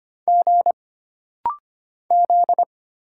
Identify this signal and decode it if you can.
Morse code, heard as a keyed tone, about 25 wpm, 700 Hz. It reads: G Z
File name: Single Character (Letter:Number:Punctuation:Prosign) - Rapid-Fire - 25wpm